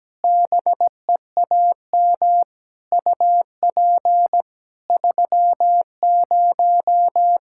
Morse code, heard as a keyed tone, about 17 wpm, 700 Hz. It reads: BEAM UP 30
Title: Practice QSO - Exchange 2 - Info - 17wpm